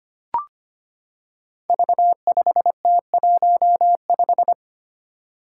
Morse code, heard as a keyed tone, about 25 wpm, 700 Hz. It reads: V5T15
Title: Five Challenging Letter-Number Combinations 25wpm